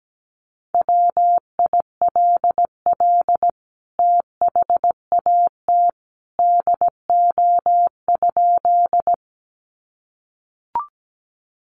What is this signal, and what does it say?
Morse code, heard as a keyed tone, about 17 wpm, 700 Hz. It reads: WILL THAT DO?
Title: Sentences from Top 100 Words 17wpm